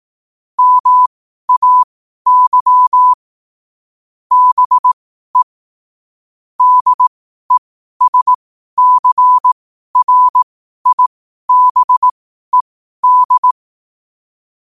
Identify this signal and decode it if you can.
Morse code, heard as a keyed tone, about 18 wpm, spaced as for 12 wpm, 1000 Hz. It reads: MAY BE DESCRIBED
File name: qotdfiles / 12wpm